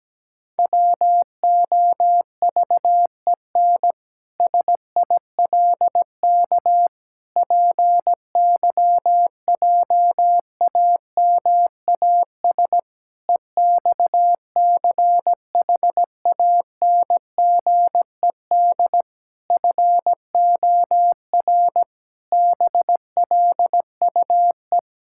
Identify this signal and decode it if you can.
Morse code, heard as a keyed tone, about 17 wpm, 700 Hz. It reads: WOVEN SILK PYJAMAS EXCHANGED FOR BLUE
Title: Pangram Challenge v2 - 17wpm